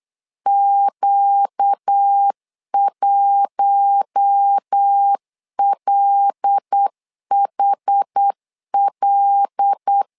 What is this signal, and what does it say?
Morse code, heard as a keyed tone, about 8 wpm, 800 Hz. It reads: Q1LHL
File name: LEZIONI / 5 LBVC / 5 L 8wpm 2831